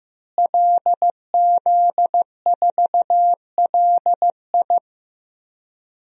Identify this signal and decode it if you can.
Morse code, heard as a keyed tone, about 15 wpm, 700 Hz. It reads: LZ4LI
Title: DXCC-100 Call Signs - 5 Characters - Repeated 3x - 15wpm